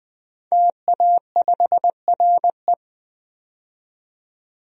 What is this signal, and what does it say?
Morse code, heard as a keyed tone, about 21 wpm, 700 Hz. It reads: TA5RE
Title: DXCC-100 Call Signs - 5 Characters - 20wpm